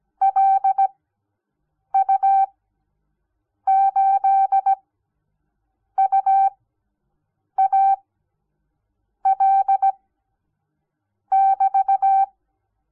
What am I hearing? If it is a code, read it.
Morse code, heard as a keyed tone, about 17 wpm, spaced as for 6 wpm, 780 Hz. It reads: LU8UAL<BT>
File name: Onlinelehrgang / morsen / DJ4UF Morsen06